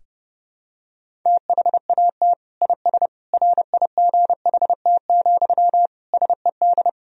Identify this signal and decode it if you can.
Morse code, heard as a keyed tone, about 30 wpm, 700 Hz. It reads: THAT IS RIGHT, SED
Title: Sentences from Top 200 Words and Abbreviations 30wpm